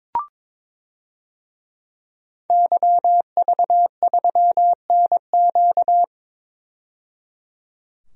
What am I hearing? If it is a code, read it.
Morse code, heard as a keyed tone, about 22 wpm, 700 Hz. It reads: YV3NQ
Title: DXCC-100 Call Signs - 5 Characters - 22wpm